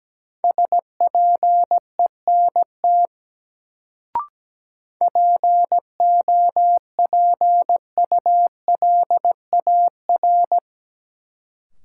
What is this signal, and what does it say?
Morse code, heard as a keyed tone, about 17 wpm, 700 Hz. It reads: SPENT POPULAR
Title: Sets of 5 Words - New Word Drill - 17wpm